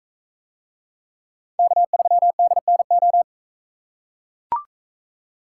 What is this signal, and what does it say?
Morse code, heard as a keyed tone, about 41 wpm, 700 Hz. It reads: K3DNO